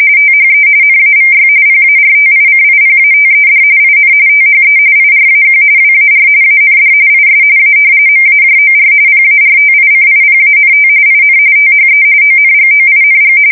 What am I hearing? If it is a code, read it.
RTTY, about 45.45 baud. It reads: THE NAVY PARTICIPATED IN THE IRAQ WAR AND IS A MAJOR PARTICIPANT IN THE ONGOING
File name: fsk170 / usnavy5